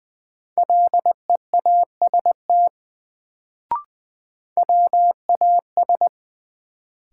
Morse code, heard as a keyed tone, about 20 wpm, 700 Hz. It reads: LEAST WAS
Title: Sets of 2 Words - New Word Drill - 20wpm